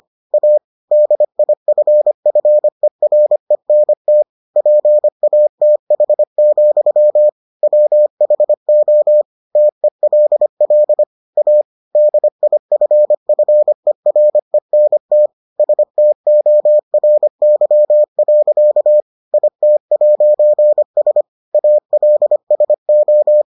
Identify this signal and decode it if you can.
Morse code, heard as a keyed tone, about 25 wpm, 600 Hz. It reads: A DIFFERENT PATH, WHO TELL A DIFFERENT STORY. IT'S ALSO